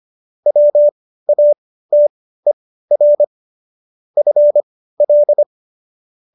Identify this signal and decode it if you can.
Morse code, heard as a keyed tone, about 25 wpm, spaced as for 15 wpm, 590 Hz. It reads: WATER FL
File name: qso222 15